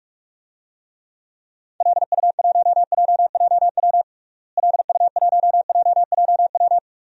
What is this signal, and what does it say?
Morse code, heard as a keyed tone, about 46 wpm, 700 Hz. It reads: LU1JJW LU1JJW
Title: DXCC-100 Call Signs - 6 Characters - Repeated 3x - 45wpm